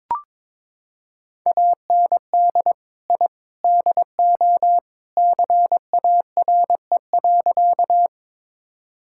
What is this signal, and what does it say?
Morse code, heard as a keyed tone, about 22 wpm, 700 Hz. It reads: AND I DO CARE.
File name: practice / 22wpm / Sentences with 3-Word Phrases - 22wpm